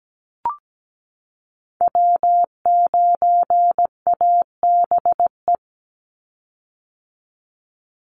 Morse code, heard as a keyed tone, about 17 wpm, 700 Hz. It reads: W9ABE
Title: USA Call Signs 17wpm